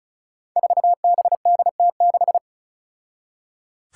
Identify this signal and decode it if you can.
Morse code, heard as a keyed tone, about 36 wpm, 700 Hz. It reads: 4BDT6
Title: Five Challenging Letter-Number Combinations 35wpm